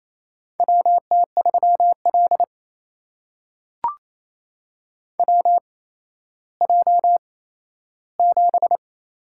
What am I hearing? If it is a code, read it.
Morse code, heard as a keyed tone, about 28 wpm, spaced as for 14 wpm, 700 Hz. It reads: WT3L WJ7
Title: USA Call Signs - 4 Characters - 8x Character Spacing - 28wpm